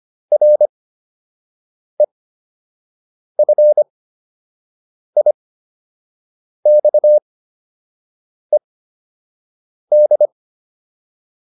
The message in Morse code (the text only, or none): REFIXED